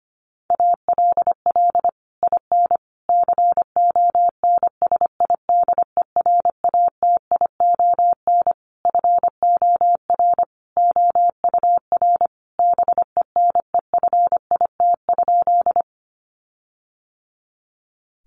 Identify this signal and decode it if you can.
Morse code, heard as a keyed tone, about 25 wpm, 700 Hz. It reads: ALL IN CONSIDERATION FOR OUR BENEFIT?
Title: Sentences from Top 1500 Words 25wpm